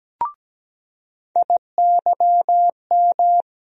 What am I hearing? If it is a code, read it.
Morse code, heard as a keyed tone, about 17 wpm, 700 Hz. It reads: IYM
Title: USA Call Sign Postfixes - 3 Letters 17wpm